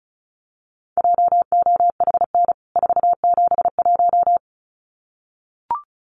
Morse code, heard as a keyed tone, about 34 wpm, 700 Hz. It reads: JOHN 471
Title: CWT Exchange 35wpm